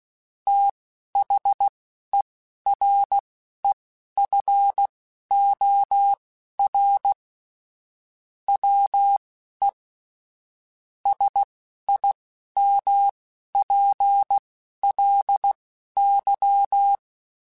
Morse code, heard as a keyed tone, about 16 wpm, spaced as for 11 wpm, 800 Hz. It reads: THEREFOR WE SIMPLY
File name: Technical Subjects / Technical Subjects-mp3s / Sw Pwr Sup @10